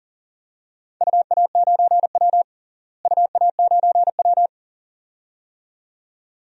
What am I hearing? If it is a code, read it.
Morse code, heard as a keyed tone, about 40 wpm, 700 Hz. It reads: UA9W UA9W
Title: DXCC-100 Call Signs - 4 Characters - Repeated 2x - 40wpm